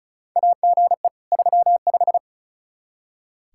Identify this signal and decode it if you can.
Morse code, heard as a keyed tone, about 36 wpm, 700 Hz. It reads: AGE 35
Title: Practice QSO - Exchange 3 - More Info - 35wpm